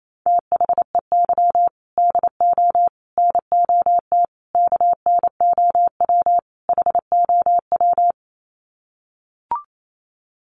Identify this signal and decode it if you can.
Morse code, heard as a keyed tone, about 28 wpm, 700 Hz. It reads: THEY DO NOT KNOW HOW
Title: Sets of 5 Words Encore2 28wpm